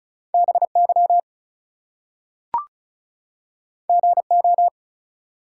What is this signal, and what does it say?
Morse code, heard as a keyed tone, about 36 wpm, 700 Hz. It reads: BY GO
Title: Two Letter Words - 35wpm